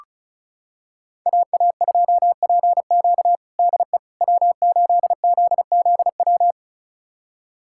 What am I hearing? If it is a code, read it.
Morse code, heard as a keyed tone, about 34 wpm, 700 Hz. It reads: AA2PQ DE W8ZZW
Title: USA Callsign DE Callsign 35wpm